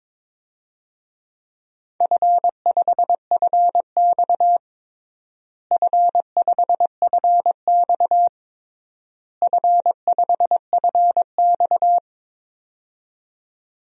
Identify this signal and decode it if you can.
Morse code, heard as a keyed tone, about 22 wpm, 700 Hz. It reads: F5FX F5FX F5FX
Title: DXCC-100 Call Signs - 4 Characters - Repeated 3x - 22wpm